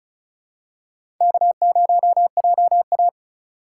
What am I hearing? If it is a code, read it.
Morse code, heard as a keyed tone, about 34 wpm, 700 Hz. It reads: K0JA
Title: USA Call Signs - 4 Characters - 4x Character Spacing - 35wpm